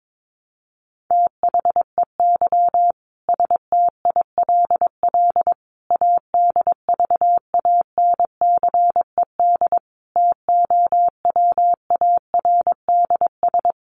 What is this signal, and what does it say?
Morse code, heard as a keyed tone, about 22 wpm, 700 Hz. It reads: THEY STILL ADVANCED TOWARDS
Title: Sentences from Top 1200 Words 22wpm